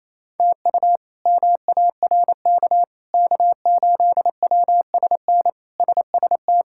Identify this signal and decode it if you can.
Morse code, heard as a keyed tone, about 28 wpm, 700 Hz. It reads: TU MARK K8WSN SST
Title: K1USN SST QSO Practice - 28wpm